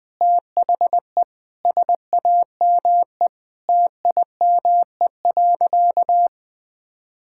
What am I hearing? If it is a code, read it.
Morse code, heard as a keyed tone, about 20 wpm, 700 Hz. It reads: THE SAME TIME.
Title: Sentences with 4-Word Phrases - 20wpm